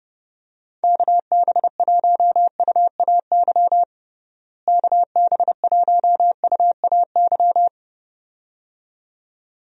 Morse code, heard as a keyed tone, about 30 wpm, 700 Hz. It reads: KB1UAY KB1UAY
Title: USA Call Signs - 6 Characters - Repeated 2x - 30wpm